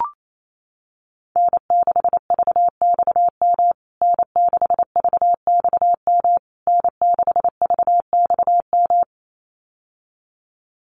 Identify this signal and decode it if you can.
Morse code, heard as a keyed tone, about 28 wpm, 700 Hz. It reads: N6VXM N6VXM N6VXM
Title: USA Call Signs Repeated 3x 28wpm